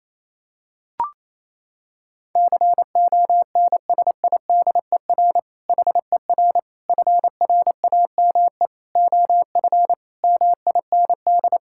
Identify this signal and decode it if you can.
Morse code, heard as a keyed tone, about 28 wpm, 700 Hz. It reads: CONSIDER HER FRAME OF MIND
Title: Sentences from Top 1000 Words - Review Part 6 - 28wpm